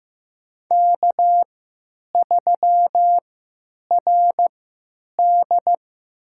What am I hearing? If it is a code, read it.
Morse code, heard as a keyed tone, about 15 wpm, spaced as for 9 wpm, 700 Hz. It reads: K3RD
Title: USA Call Signs - 3x Character Spacing - 15wpm